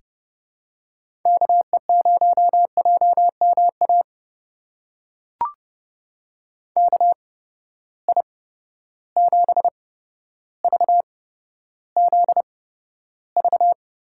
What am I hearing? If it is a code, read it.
Morse code, heard as a keyed tone, about 32 wpm, spaced as for 13 wpm, 700 Hz. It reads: KE0JMA KI7VZV
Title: USA Call Signs - 6 Characters - 8x Character Spacing - 30wpm